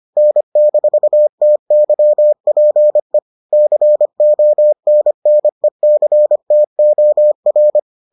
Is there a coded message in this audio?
Yes: Morse code from 0.2 to 7.8 s, heard as a keyed tone, about 25 wpm, 600 Hz. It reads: N-TYPE CONNECTOR